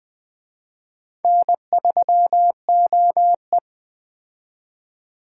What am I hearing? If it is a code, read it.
Morse code, heard as a keyed tone, about 20 wpm, 700 Hz. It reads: N3OE
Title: Speed Racing - USA Call Signs 17wpm